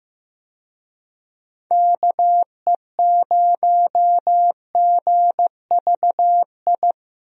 Morse code, heard as a keyed tone, about 15 wpm, 700 Hz. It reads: KE0GVI
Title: USA Call Signs - 6 Characters - Repeated 2x - 15wpm